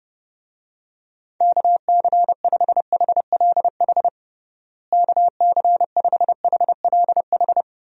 Morse code, heard as a keyed tone, about 30 wpm, 700 Hz. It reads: KC5HLH KC5HLH
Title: USA Call Signs - 6 Characters - Repeated 2x - 30wpm